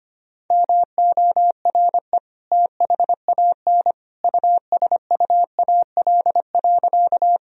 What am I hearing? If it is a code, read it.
Morse code, heard as a keyed tone, about 25 wpm, 700 Hz. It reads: MORE THAN USUAL.
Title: Sentences from Top 1000 Words - Review Part 2 - 25wpm